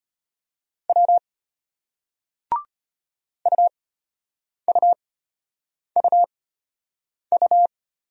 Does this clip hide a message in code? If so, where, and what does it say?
Yes: Morse code from 0.9 to 7.7 s, heard as a keyed tone, about 34 wpm, spaced as for 8 wpm, 690 Hz. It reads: W UUUU